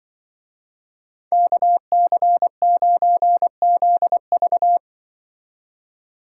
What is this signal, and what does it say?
Morse code, heard as a keyed tone, about 24 wpm, 700 Hz. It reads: KC9ZV
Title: Speed Racing - USA 2x2 Call Signs 20wpm